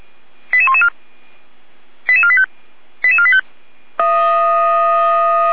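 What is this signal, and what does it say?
Five-tone selective call (ZVEI), heard as quick runs of five tones, tones of about 70 ms. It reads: 88114 88485 88485
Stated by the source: ZVEI five-tone selective call 88113 received on SDR.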